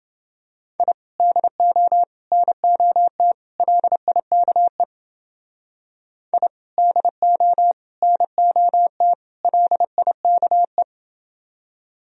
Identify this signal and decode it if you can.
Morse code, heard as a keyed tone, about 28 wpm, 700 Hz. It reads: I DO NOT LIKE I DO NOT LIKE